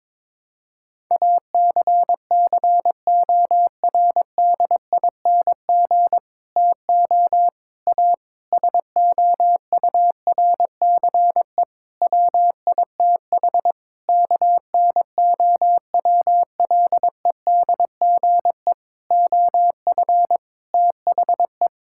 Morse code, heard as a keyed tone, about 22 wpm, 700 Hz. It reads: ACCORDING TO A SOURCE WITH KNOWLEDGE OF THE
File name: Sets of 9 Words - 22wpm